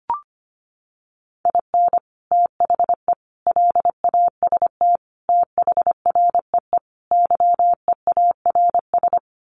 Morse code, heard as a keyed tone, about 25 wpm, 700 Hz. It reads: IN THE LAST THREE YEARS
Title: Sets of 5 Words Encore2 25wpm